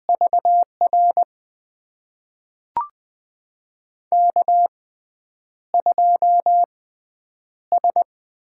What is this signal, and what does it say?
Morse code, heard as a keyed tone, about 20 wpm, spaced as for 9 wpm, 700 Hz. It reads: VR K2S
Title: USA Call Signs - 5 Characters - 6x Character Spacing - 20wpm